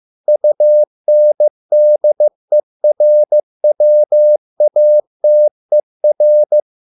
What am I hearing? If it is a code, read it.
Morse code, heard as a keyed tone, about 15 wpm, 600 Hz. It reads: UNDERWATER